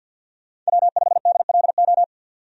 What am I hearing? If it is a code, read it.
Morse code, heard as a keyed tone, about 48 wpm, 700 Hz. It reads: W5DLO
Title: USA Call Signs - 5 Characters - 50wpm